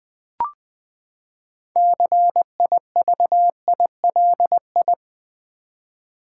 Morse code, heard as a keyed tone, about 20 wpm, 700 Hz. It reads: CIVILI
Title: Build A Word - 6 Letters At A Time - 12-Letter Words - 20wpm